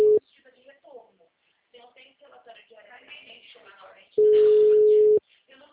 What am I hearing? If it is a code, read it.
Ringback tone (Europe, 425 Hz), with speech.